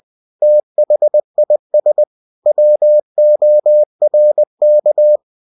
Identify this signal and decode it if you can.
Morse code, heard as a keyed tone, about 20 wpm, 600 Hz. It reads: THIS WORK
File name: cw-books / winnie-the-pooh / winnie-20wpm-page-0112